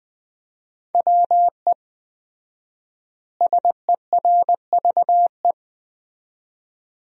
Morse code, heard as a keyed tone, about 20 wpm, 700 Hz. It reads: WE SERVE
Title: 4x Word-Spacing - Sentences from Top 400 Words - 20wpm